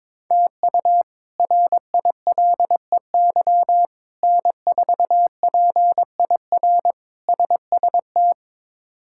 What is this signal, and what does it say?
Morse code, heard as a keyed tone, about 22 wpm, 700 Hz. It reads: TU RILEY N4PIR SST
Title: K1USN SST QSO Practice - 22wpm